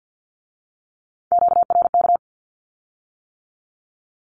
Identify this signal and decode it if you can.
Morse code, heard as a keyed tone, about 50 wpm, 700 Hz. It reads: QRK
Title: Q-Codes with Spoken Meaning - 50wpm